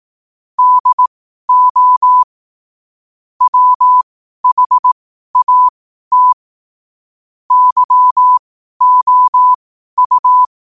Morse code, heard as a keyed tone, about 18 wpm, spaced as for 12 wpm, 1000 Hz. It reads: DO WHAT YOU
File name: qotdfiles / 12wpm